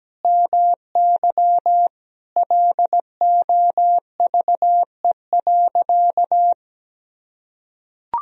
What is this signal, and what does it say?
Morse code, heard as a keyed tone, about 17 wpm, 700 Hz. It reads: MY LOVE.
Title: Sentences from Top 400 Words and Abbreviations 17wpm